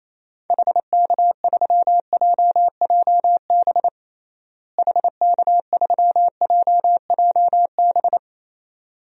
Morse code, heard as a keyed tone, about 28 wpm, 700 Hz. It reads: HK3JJB HK3JJB
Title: DXCC-100 Call Signs - 6 Characters - Repeated 2x - 28wpm